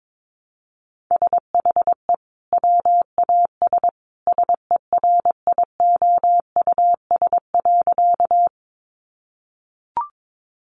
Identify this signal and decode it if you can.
Morse code, heard as a keyed tone, about 22 wpm, 700 Hz. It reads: SHE WAS SERIOUS.
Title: Sentences from Top 1100 Words 22wpm